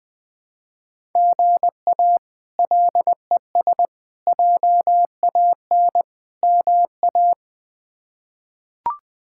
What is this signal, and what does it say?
Morse code, heard as a keyed tone, about 20 wpm, 700 Hz. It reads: GA LES JAN MA
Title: K1USN SST QSO Practice - 20wpm